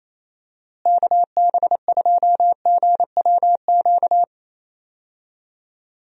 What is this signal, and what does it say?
Morse code, heard as a keyed tone, about 27 wpm, 700 Hz. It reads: KB2GWQ
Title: USA Call Signs - 6 Characters - Repeated 3x - 28wpm